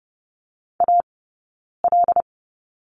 Morse code, heard as a keyed tone, about 30 wpm, spaced as for 9 wpm, 690 Hz. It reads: AL
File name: Lesson 19d - Letter w - Two Characters 30wpm